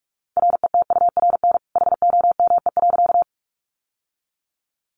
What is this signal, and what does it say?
Morse code, heard as a keyed tone, about 44 wpm, 700 Hz. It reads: RETURN HOME.
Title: Sentences from Top 800 Words 45wpm